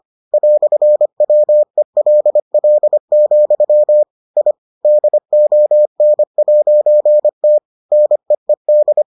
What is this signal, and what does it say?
Morse code, heard as a keyed tone, about 25 wpm, 600 Hz. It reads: "WELL, I DON'T NEED